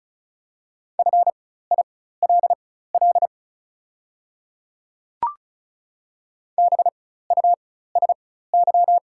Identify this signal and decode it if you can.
Morse code, heard as a keyed tone, about 35 wpm, spaced as for 16 wpm, 690 Hz. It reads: FILL BUSY
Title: 4x Character Spacing - Top 500 Words - 35wpm